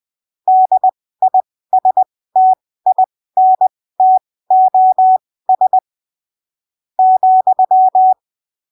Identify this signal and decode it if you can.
Morse code, heard as a keyed tone, about 20 wpm, spaced as for 15 wpm, 750 Hz. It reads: DISTINTOS ,
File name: arquivos / texto / texto 22 20ppm